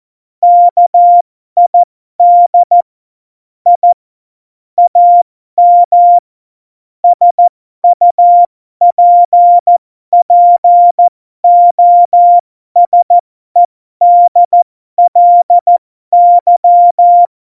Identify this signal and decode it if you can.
Morse code, heard as a keyed tone, about 14 wpm, spaced as for 12 wpm, 700 Hz. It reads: KID I AM SUPPOSEDLY
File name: Morsekurs / Witze